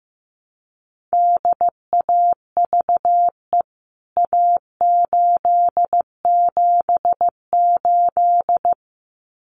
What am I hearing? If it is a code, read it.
Morse code, heard as a keyed tone, about 15 wpm, 700 Hz. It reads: DAVE A878
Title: CWT Exchange 15wpm